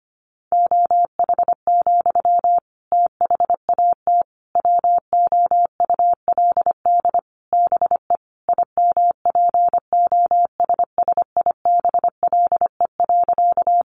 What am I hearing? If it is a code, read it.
Morse code, heard as a keyed tone, about 25 wpm, 700 Hz. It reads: OH, THAT WOULD BE IMPOSSIBLE.
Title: Sentences from Top 1100 Words 25wpm